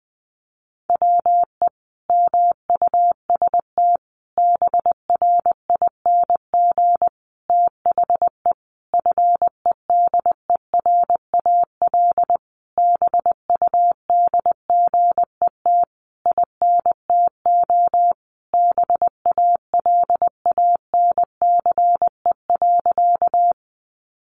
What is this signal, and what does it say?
Morse code, heard as a keyed tone, about 20 wpm, 700 Hz. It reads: WE MUST BRING THE FEDERAL BUDGET INTO BALANCE.
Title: Sentences from Top 1200 Words 20wpm